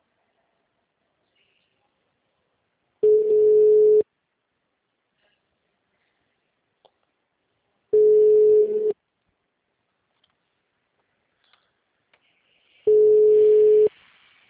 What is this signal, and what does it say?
Ringback tone (Europe, 425 Hz).
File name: out- -2012- - - .17278